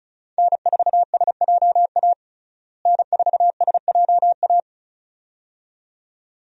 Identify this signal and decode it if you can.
Morse code, heard as a keyed tone, about 36 wpm, 700 Hz. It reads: N4SJA N4SJA
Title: USA Call Signs - 5 Characters - Repeated 2x - 35wpm